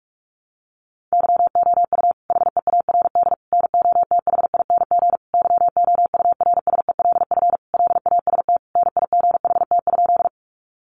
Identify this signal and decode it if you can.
Morse code, heard as a keyed tone, about 44 wpm, 700 Hz. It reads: YOU HEARD NOTHING YOURSELF LAST NIGHT?
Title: Sentences from Top 600 Words 45wpm